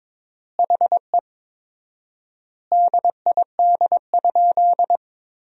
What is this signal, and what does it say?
Morse code, heard as a keyed tone, about 22 wpm, 700 Hz. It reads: HE DID?
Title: 4x Word-Spacing - Sentences from Top 100 Words - 22wpm